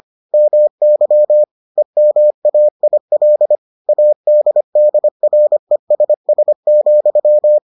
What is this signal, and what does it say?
Morse code, heard as a keyed tone, about 25 wpm, 600 Hz. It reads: MY EMAIL ADDRESS,